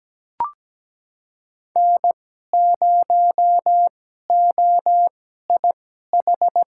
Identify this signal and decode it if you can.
Morse code, heard as a keyed tone, about 17 wpm, spaced as for 12 wpm, 700 Hz. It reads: N0OIH
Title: USA Call Signs - 5 Characters - 2x Character Spacing - 17wpm